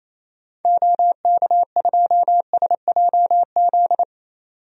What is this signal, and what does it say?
Morse code, heard as a keyed tone, about 28 wpm, 700 Hz. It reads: OK2SJZ
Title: DXCC-100 Call Signs - 6 Characters - 28wpm